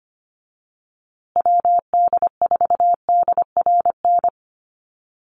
Morse code, heard as a keyed tone, about 25 wpm, 700 Hz. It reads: WD4DRN
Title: USA Call Signs Repeated 2x 25wpm